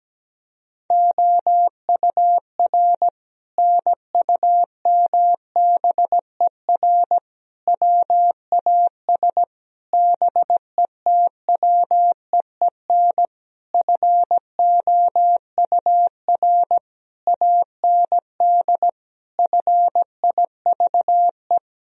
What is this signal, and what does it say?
Morse code, heard as a keyed tone, about 17 wpm, 700 Hz. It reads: OUR NUMBER WAS BETWEEN FOUR AND FIVE
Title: Sentences from Top 1000 Words - Review Part 2 - 17wpm